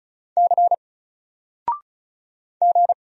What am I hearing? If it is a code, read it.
Morse code, heard as a keyed tone, about 34 wpm, 700 Hz. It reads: C G
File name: Single-Letter-Number-35wpm